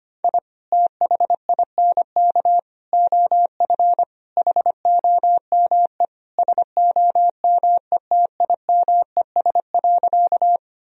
Morse code, heard as a keyed tone, about 25 wpm, 700 Hz. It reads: I THINK OF HOME SOMETIMES.
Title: Sentences from Top 1000 Words - Review Part 3 - 25wpm